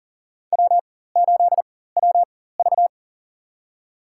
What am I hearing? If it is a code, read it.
Morse code, heard as a keyed tone, about 40 wpm, spaced as for 19 wpm, 700 Hz. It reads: W8WV
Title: USA Call Signs - 4 Characters - 4x Character Spacing - 40wpm